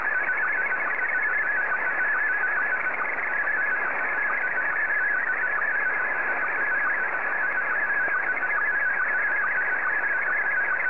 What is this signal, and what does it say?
SSTV (slow-scan television), Scottie 1.